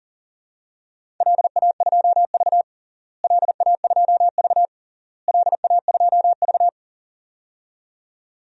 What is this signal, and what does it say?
Morse code, heard as a keyed tone, about 40 wpm, 700 Hz. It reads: LA2V LA2V LA2V
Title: DXCC-100 Call Signs - 4 Characters - Repeated 3x - 40wpm